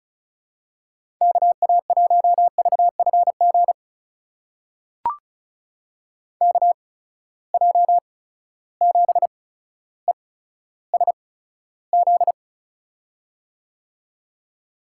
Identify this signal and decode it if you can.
Morse code, heard as a keyed tone, about 34 wpm, spaced as for 15 wpm, 700 Hz. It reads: KA1VFG KJ7ESZ
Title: USA Call Signs - 6 Characters - 8x Character Spacing - 35wpm